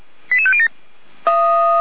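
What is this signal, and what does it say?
Five-tone selective call (ZVEI), heard as quick runs of five tones, tones of about 70 ms. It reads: 88507
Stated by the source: Five-tone ZVEI selective call received on SDR.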